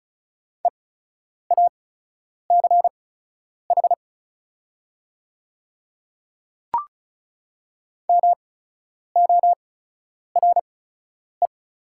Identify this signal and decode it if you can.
Morse code, heard as a keyed tone, about 34 wpm, spaced as for 10 wpm, 700 Hz. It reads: EACH MORE
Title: 8x Character Spacing - Top 100 Words - 35wpm